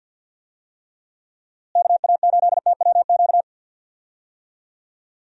Morse code, heard as a keyed tone, about 52 wpm, 700 Hz. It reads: KA8TWQ
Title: USA Call Signs - 6 Characters - Repeated 3x - 50wpm